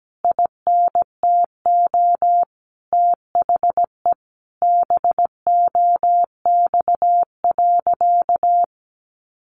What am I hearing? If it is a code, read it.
Morse code, heard as a keyed tone, about 17 wpm, 700 Hz. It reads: INTO THE BOX.
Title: Sentences from Top 1000 Words 17wpm